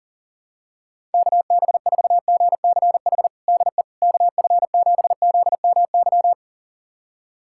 Morse code, heard as a keyed tone, about 37 wpm, 700 Hz. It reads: KB4GCH DE KF7ZMY
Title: USA Callsign DE Callsign 40wpm